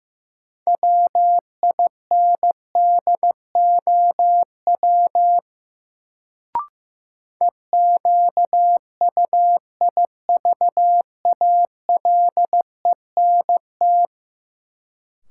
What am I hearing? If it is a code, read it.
Morse code, heard as a keyed tone, about 15 wpm, 700 Hz. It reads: WINDOW EQUIVALENT
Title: Sentences from Top 1600 Words - New Word Drill Alt - 15wpm